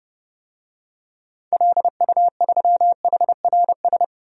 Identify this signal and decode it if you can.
Morse code, heard as a keyed tone, about 32 wpm, 700 Hz. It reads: LU3HRS
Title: DXCC-100 Call Signs - 6 Characters - 30wpm